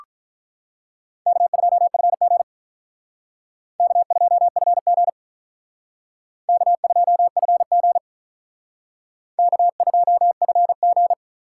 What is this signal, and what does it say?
Morse code, heard as a keyed tone, about 45 wpm, 700 Hz. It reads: K2FG K2FG K2FG K2FG